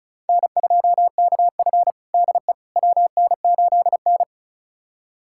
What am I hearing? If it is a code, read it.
Morse code, heard as a keyed tone, about 36 wpm, 700 Hz. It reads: N2KF DE WN8N
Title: USA Callsign DE Callsign - 4 Characters - 35wpm